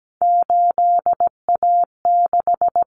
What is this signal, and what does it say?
Morse code, heard as a keyed tone, about 17 wpm, 700 Hz. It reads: 8A6
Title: Common Cut Numbers - 3-Digit 17wpm